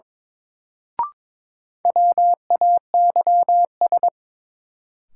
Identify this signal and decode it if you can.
Morse code, heard as a keyed tone, about 22 wpm, 700 Hz. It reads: WAYS
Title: Sets of 5 Words Encore - New Word Drill - 22wpm